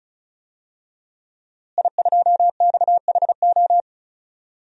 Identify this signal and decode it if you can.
Morse code, heard as a keyed tone, about 36 wpm, 700 Hz. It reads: I2XHO
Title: DXCC-100 Call Signs - 5 Characters - Repeated 3x - 35wpm